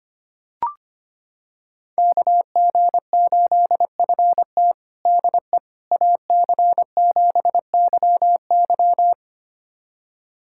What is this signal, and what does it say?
Morse code, heard as a keyed tone, about 25 wpm, 700 Hz. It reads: KG8FT DE AC7YY
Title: USA Callsign DE Callsign - 5 Characters - 25wpm